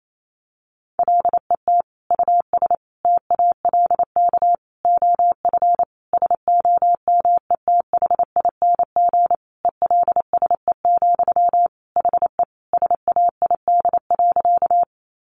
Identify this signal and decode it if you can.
Morse code, heard as a keyed tone, about 28 wpm, 700 Hz. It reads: LET US TALK OF SOMETHING ELSE, HE SAID.
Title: Sentences from Top 800 Words 28wpm